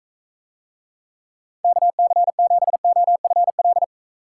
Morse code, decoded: KC7OFL

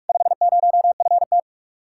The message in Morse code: 50FT

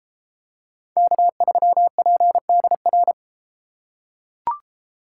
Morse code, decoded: K3PDR

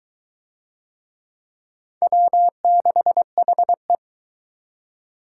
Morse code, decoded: W6HE

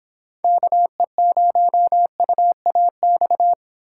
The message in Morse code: KE0UAX

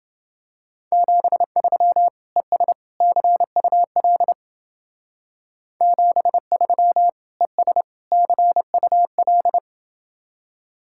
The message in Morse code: 73 ES CUL 73 ES CUL

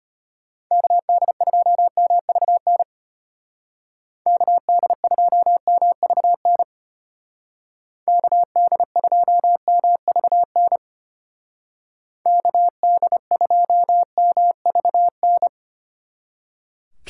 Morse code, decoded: KD2MVN KD2MVN KD2MVN KD2MVN

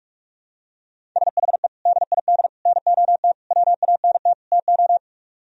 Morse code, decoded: SHE DID NOT WANT TO

